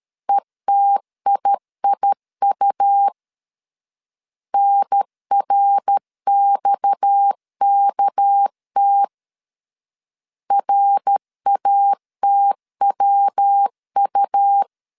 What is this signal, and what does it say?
Morse code, heard as a keyed tone, about 13 wpm, 800 Hz. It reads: ETIIU NRXKT RATWU
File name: CW / DET / 1 6 1500C S B 12wpm 3431